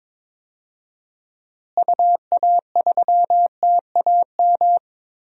Morse code, heard as a keyed tone, about 22 wpm, 700 Hz. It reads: UA3TAM
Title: DXCC-100 Call Signs - 6 Characters - Repeated 2x - 22wpm